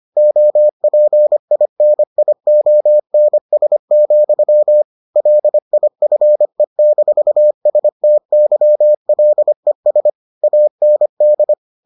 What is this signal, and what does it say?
Morse code, heard as a keyed tone, about 25 wpm, 600 Hz. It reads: OPINIONS, LIFE-STYLES AND